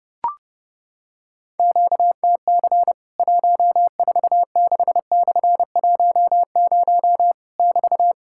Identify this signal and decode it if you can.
Morse code, heard as a keyed tone, about 30 wpm, 700 Hz. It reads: QTC 146/10 <BT>
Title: Worked All Europe DX Contest - QTC Exchange - 30wpm